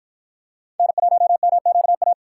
Morse code, heard as a keyed tone, about 50 wpm, 700 Hz. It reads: N1MQA